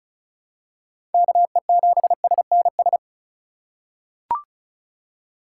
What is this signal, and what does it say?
Morse code, heard as a keyed tone, about 36 wpm, 700 Hz. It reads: KE7SNS
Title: USA Call Signs - 6 Characters - 3x Character Spacing - 35wpm